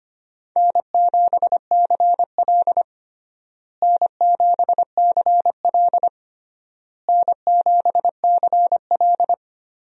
Morse code, heard as a keyed tone, about 25 wpm, 700 Hz. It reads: N7CL N7CL N7CL
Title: USA Call Signs - 4 Characters - Repeated 3x - 25wpm